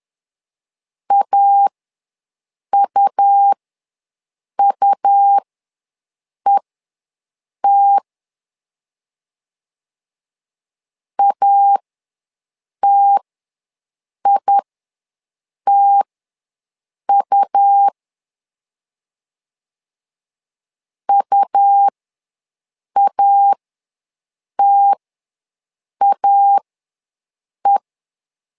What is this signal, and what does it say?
Morse code, heard as a keyed tone, about 11 wpm, spaced as for 6 wpm, 800 Hz. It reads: AUUET ATITU UATAE
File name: LEZIONI / 2 UHDN / 2 U ws 10wpm 2017